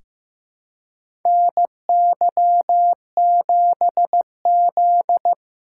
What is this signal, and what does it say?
Morse code, heard as a keyed tone, about 15 wpm, 700 Hz. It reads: NY7Z
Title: K1USN SST QSO Practice - 15wpm